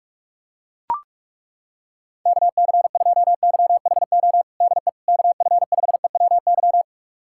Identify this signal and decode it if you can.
Morse code, heard as a keyed tone, about 42 wpm, 700 Hz. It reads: KC2YHO DE KF5EWY
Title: USA Callsign DE Callsign - 6 Characters - 45wpm